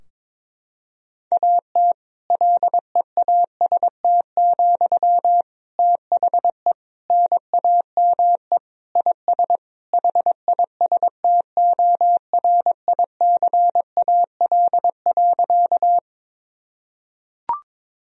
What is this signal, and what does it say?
Morse code, heard as a keyed tone, about 22 wpm, 700 Hz. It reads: AT LEAST, THE NAME IS HISTORICAL.
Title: Sentences from Top 1000 Words - Review Part 4 - 22wpm